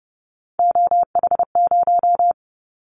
Morse code, heard as a keyed tone, about 30 wpm, 700 Hz. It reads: OH0
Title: Callsign Prefixes for Top 100 DXCC - 30wpm